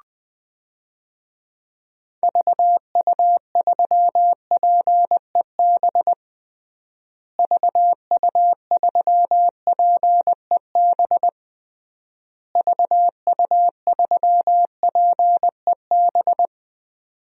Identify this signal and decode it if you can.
Morse code, heard as a keyed tone, about 19 wpm, 700 Hz. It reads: VU3PEB VU3PEB VU3PEB
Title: DXCC-100 Call Signs - 6 Characters - Repeated 3x - 20wpm